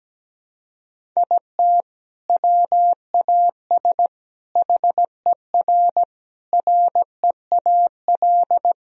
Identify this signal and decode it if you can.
Morse code, heard as a keyed tone, about 17 wpm, 700 Hz. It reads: IT WAS HER REAL
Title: Sentences from Top 1000 Words - Review Part 1 - 17wpm